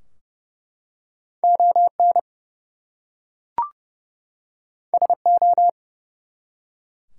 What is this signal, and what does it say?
Morse code, heard as a keyed tone, about 31 wpm, 700 Hz. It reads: ON SO
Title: Two Letter Words - 30wpm